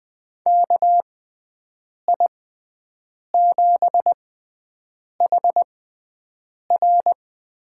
Morse code, heard as a keyed tone, about 21 wpm, spaced as for 7 wpm, 700 Hz. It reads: KI7HR
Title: USA Call Signs - 6 Characters - 6x Character Spacing - 20wpm